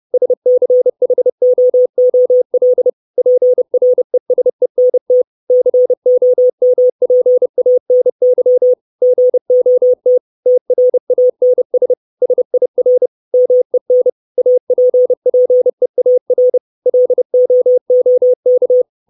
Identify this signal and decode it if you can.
Morse code, heard as a keyed tone, about 30 wpm, 500 Hz. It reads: SCHOOL PRESENT COMPANY GOT TRANS SIR MEN APPEAR LOOK
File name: 632 Words 30WPM